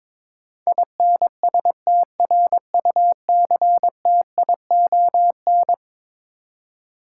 Morse code, heard as a keyed tone, about 22 wpm, 700 Hz. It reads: INSTRUCTION
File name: practice / 22wpm / Eleven Letter Words - 22wpm